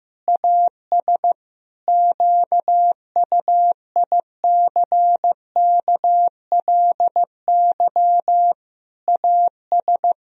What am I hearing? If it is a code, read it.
Morse code, heard as a keyed tone, about 15 wpm, 700 Hz. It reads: AS QUICKLY AS
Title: Sentences from Top 1000 Words - Review Part 3 - 15wpm